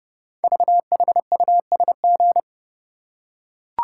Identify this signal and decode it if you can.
Morse code, heard as a keyed tone, about 28 wpm, 700 Hz. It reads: VHUSG
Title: Code Groups - 5 Letters - 30wpm